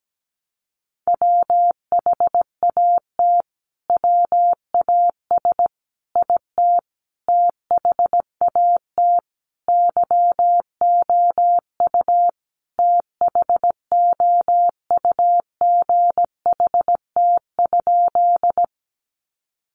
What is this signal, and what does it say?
Morse code, heard as a keyed tone, about 17 wpm, 700 Hz. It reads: WHAT WAS IT THAT YOU THOUGHT?
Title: Sentences from Top 300 Words 17wpm